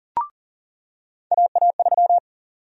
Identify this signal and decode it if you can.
Morse code, heard as a keyed tone, about 40 wpm, 700 Hz. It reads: AA3
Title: USA Call Sign Prefixes - 40wpm